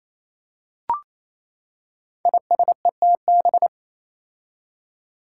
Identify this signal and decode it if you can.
Morse code, heard as a keyed tone, about 27 wpm, 700 Hz. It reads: ISETB
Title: Five Challenging Letter-Number Combinations 28wpm